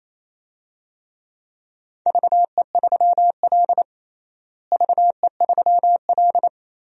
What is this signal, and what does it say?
Morse code, heard as a keyed tone, about 28 wpm, 700 Hz. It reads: VE3L VE3L
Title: DXCC-100 Call Signs - 4 Characters - Repeated 3x - 28wpm